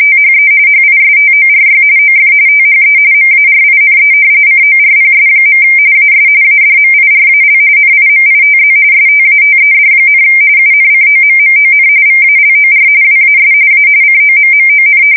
RTTY, about 45.45 baud. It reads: THE HISTORY OF THE MARINE CORPS. AS ADMIRAL NIMITZ STATED, UNCOMMON VALOR WAS A COMMON
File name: fsk170 / battleshipnj